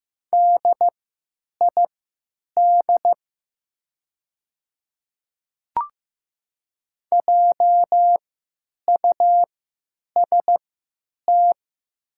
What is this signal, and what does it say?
Morse code, heard as a keyed tone, about 15 wpm, spaced as for 8 wpm, 700 Hz. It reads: DID JUST
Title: 3x Character Spacing - Top 200 Words - 15wpm